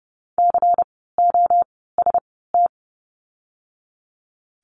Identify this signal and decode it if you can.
Morse code, heard as a keyed tone, about 30 wpm, 700 Hz. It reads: C O S T
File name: Lesson 16f - Letter c - Words 30wpm